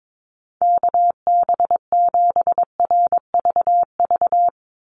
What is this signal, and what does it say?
Morse code, heard as a keyed tone, about 22 wpm, 700 Hz. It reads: KB7RVV
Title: USA Call Signs 22wpm